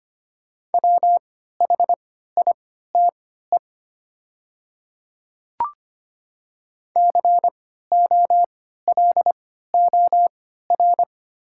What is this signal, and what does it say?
Morse code, heard as a keyed tone, about 25 wpm, spaced as for 14 wpm, 690 Hz. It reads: WHITE COLOR
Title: 3x Character Spacing - Top 500 Words - 25wpm